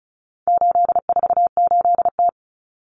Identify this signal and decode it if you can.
Morse code, heard as a keyed tone, about 34 wpm, 700 Hz. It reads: 848T
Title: Common Cut Numbers - 4-Digit 35wpm